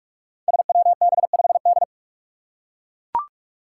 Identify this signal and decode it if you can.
Morse code, heard as a keyed tone, about 44 wpm, 700 Hz. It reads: SWB5D
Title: Five Challenging Letter-Number Combinations 45wpm